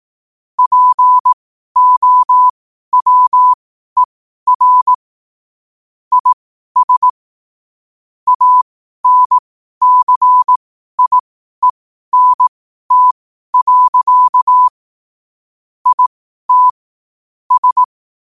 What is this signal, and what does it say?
Morse code, heard as a keyed tone, about 18 wpm, spaced as for 12 wpm, 1000 Hz. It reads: POWER IS ANCIENT. IT S
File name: qotdfiles / 12wpm